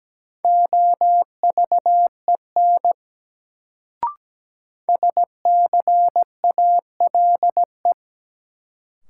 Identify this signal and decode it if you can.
Morse code, heard as a keyed tone, about 17 wpm, 700 Hz. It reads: OVEN SCALE
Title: Sentences from Top 1000 Words - Review Part 4 - New Word Drill - 17wpm